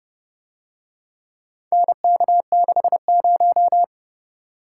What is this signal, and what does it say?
Morse code, heard as a keyed tone, about 30 wpm, 700 Hz. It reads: NK60